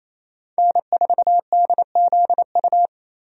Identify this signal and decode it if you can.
Morse code, heard as a keyed tone, about 28 wpm, 700 Hz. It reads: N4DZU
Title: USA Call Signs - 5 Characters - Repeated 2x - 28wpm